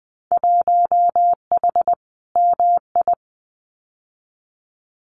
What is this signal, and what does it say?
Morse code, heard as a keyed tone, about 20 wpm, 700 Hz. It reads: 1H MI
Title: Winter Field Day Exchanges 20wpm